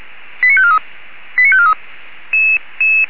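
Five-tone selective call (ZVEI), heard as quick runs of five tones, tones of about 70 ms. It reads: 89543 89543
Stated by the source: ZVEI five-tone call received by SDR.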